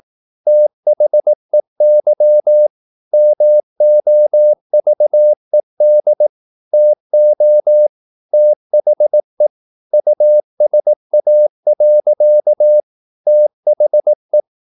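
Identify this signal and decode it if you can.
Morse code, heard as a keyed tone, about 18 wpm, 600 Hz. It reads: THEY MOVED TO THE USA. THE
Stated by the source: Morse practice at 18 wpm, English text, from CW med Gnister.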